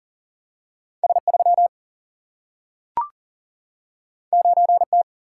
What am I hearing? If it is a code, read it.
Morse code, heard as a keyed tone, about 38 wpm, 700 Hz. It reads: S 3 9 T
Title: Single-Letter Single-Number 40wpm